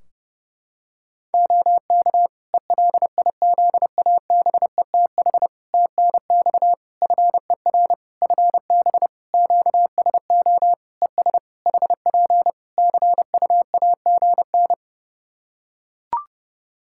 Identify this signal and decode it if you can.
Morse code, heard as a keyed tone, about 30 wpm, 700 Hz. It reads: OK ELIZABETH TNX FER FB QSO ES HP CUAGN
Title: Practice QSO - Exchange 4 - Ending - 30wpm